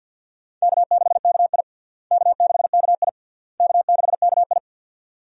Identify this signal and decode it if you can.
Morse code, heard as a keyed tone, about 53 wpm, 700 Hz. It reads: K6KI K6KI K6KI
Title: USA Call Signs - 4 Characters - Repeated 3x - 50wpm